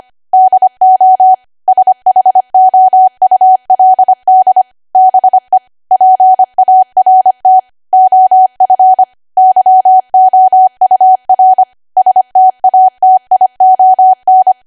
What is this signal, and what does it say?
Morse code, heard as a keyed tone, about 25 wpm, 750 Hz. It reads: DO SHOULD BE PART OF YOUR STATION
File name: Archive / 25 WPM / 25WPM